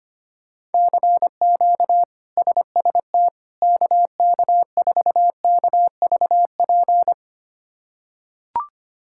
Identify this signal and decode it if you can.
Morse code, heard as a keyed tone, about 25 wpm, 700 Hz. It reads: CQ SST KK4KVP
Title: K1USN SST QSO Practice - 25wpm